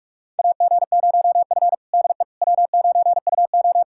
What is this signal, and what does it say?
Morse code, heard as a keyed tone, about 41 wpm, 700 Hz. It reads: AG0F DE W9UO
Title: USA Callsign DE Callsign - 4 Characters - 45wpm